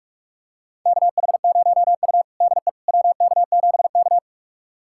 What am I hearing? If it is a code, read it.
Morse code, heard as a keyed tone, about 46 wpm, 700 Hz. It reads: KH0U DE WK7K
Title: USA Callsign DE Callsign - 4 Characters - 45wpm